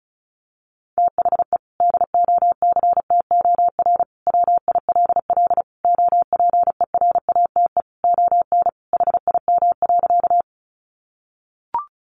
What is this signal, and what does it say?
Morse code, heard as a keyed tone, about 34 wpm, 700 Hz. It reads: THE DOCTOR WILL OPERATE ON HIM.
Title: Sentences from Top 1500 Words 35wpm